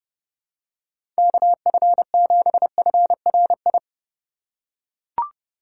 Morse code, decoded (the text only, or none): KF7FRI